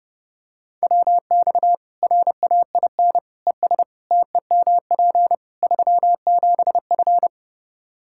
Morse code, heard as a keyed tone, about 32 wpm, 700 Hz. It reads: WX RAIN ES TEMP 37F
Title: Practice QSO - Exchange 2 - Info - 30wpm